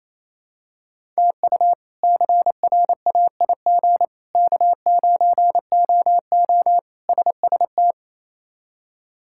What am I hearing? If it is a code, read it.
Morse code, heard as a keyed tone, about 28 wpm, 700 Hz. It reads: TU CRAIG K9OO SST
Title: K1USN SST QSO Practice - 28wpm